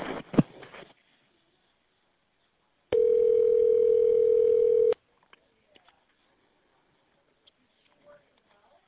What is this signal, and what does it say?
Ringback tone (North America), 440 + 480 Hz.